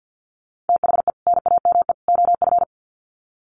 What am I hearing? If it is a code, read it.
Morse code, heard as a keyed tone, about 50 wpm, 700 Hz. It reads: THE NAME OF
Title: Sets of 3 Words 50wpm